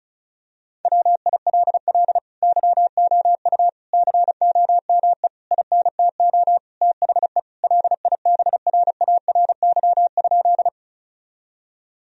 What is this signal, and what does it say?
Morse code, heard as a keyed tone, about 36 wpm, 700 Hz. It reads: WILL YOU COME INTO THE LIBRARY?
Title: Sentences from Top 1000 Words - Review Part 4 - 35wpm